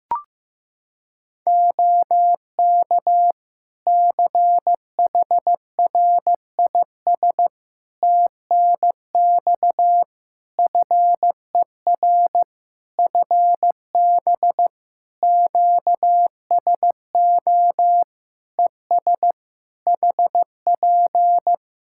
Morse code, heard as a keyed tone, about 16 wpm, 700 Hz. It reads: OK CHRIS TNX FER FB QSO ES HP
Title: Practice QSO - Exchange 4 - Ending - 15wpm